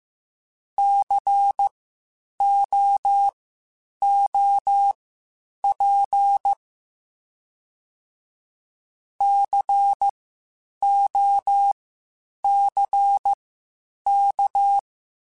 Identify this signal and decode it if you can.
Morse code, heard as a keyed tone, about 15 wpm, spaced as for 8 wpm, 800 Hz. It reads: COOP COCK
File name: Spaced Alphas / Spaced Alphas mp3s / Cs sp @13